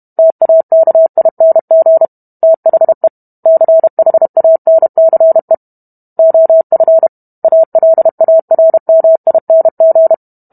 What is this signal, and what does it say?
Morse code, heard as a keyed tone, about 32 wpm, 650 Hz. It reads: TAKING THE CHANCE OF ALARMING